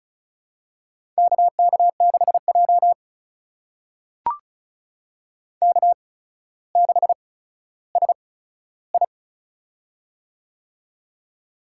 Morse code, heard as a keyed tone, about 36 wpm, spaced as for 15 wpm, 700 Hz. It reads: KK6J K6SI
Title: USA Call Signs - 4 Characters - 8x Character Spacing - 35wpm